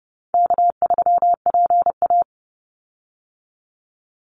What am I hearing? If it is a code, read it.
Morse code, heard as a keyed tone, about 30 wpm, 700 Hz. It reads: K3PA
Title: Easy Call Signs Repeated 3x - 30wpm